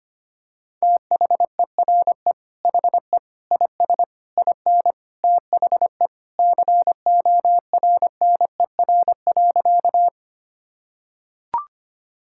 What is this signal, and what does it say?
Morse code, heard as a keyed tone, about 25 wpm, 700 Hz. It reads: THERE HE IS IN THE CORNER.
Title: Sentences from Top 1600 Words 25wpm